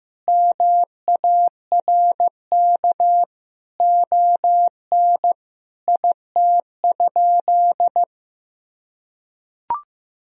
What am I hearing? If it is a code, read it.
Morse code, heard as a keyed tone, about 15 wpm, 700 Hz. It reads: MARK ON IT?
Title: Sentences from Top 1000 Words - Review Part 1 - 15wpm